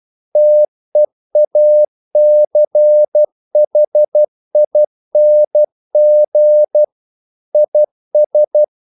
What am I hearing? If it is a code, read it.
Morse code, heard as a keyed tone, about 12 wpm, 600 Hz. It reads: TEACHING IS
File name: CW / Practice / EnglishQuotations0312